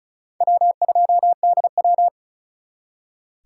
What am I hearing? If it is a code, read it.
Morse code, heard as a keyed tone, about 44 wpm, 700 Hz. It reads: W2DW